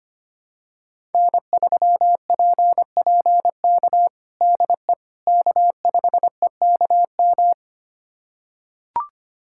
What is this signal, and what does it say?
Morse code, heard as a keyed tone, about 25 wpm, 700 Hz. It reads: N3PPK DE K5EKM
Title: USA Callsign DE Callsign - 5 Characters - 25wpm